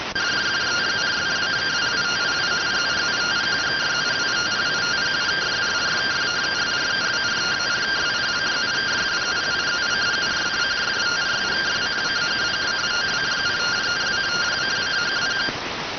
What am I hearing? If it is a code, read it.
Two-tone teleprinter signal, about 170 Hz shift.